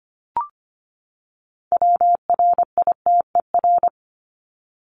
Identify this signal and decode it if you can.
Morse code, heard as a keyed tone, about 25 wpm, 700 Hz. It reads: WRITER
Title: Top 1751-2000 Words 25wpm